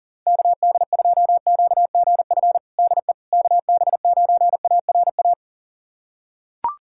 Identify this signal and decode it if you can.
Morse code, heard as a keyed tone, about 37 wpm, 700 Hz. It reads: KD2QGF DE KB9ARA
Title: USA Callsign DE Callsign - 6 Characters - 40wpm